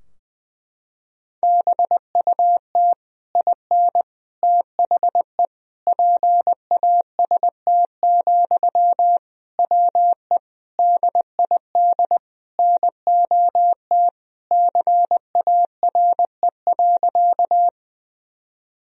Morse code, heard as a keyed tone, about 20 wpm, 700 Hz. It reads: BUT IN THE PAST, WE DID NOT CARE.
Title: Sentences from Top 1000 Words - Review Part 3 - 20wpm